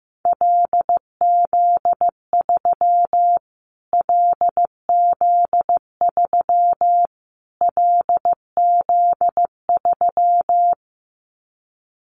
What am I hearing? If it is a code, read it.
Morse code, heard as a keyed tone, about 15 wpm, 700 Hz. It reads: LZ3 LZ3 LZ3
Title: Callsign Prefixes for Top 25 DXCC - Repeated 3x - 15wpm